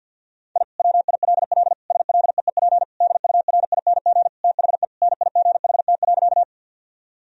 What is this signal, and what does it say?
Morse code, heard as a keyed tone, about 50 wpm, 700 Hz. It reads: I WILL SLEEP DURING THE NIGHT.
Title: Sentences from Top 1000 Words - Review Part 2 - 50wpm